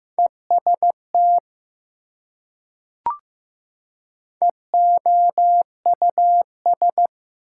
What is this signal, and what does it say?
Morse code, heard as a keyed tone, about 15 wpm, 700 Hz. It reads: EST EOUS
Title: Prefixes and Suffixes 15wpm